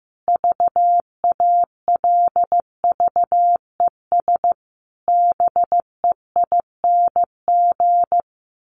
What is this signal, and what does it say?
Morse code, heard as a keyed tone, about 15 wpm, 700 Hz. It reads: VALVES BEING
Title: A Sound Of Thunder 15wpm